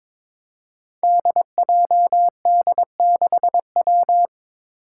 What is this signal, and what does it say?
Morse code, heard as a keyed tone, about 22 wpm, 700 Hz. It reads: DJD6W
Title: Five Challenging Letter-Number Combinations 22wpm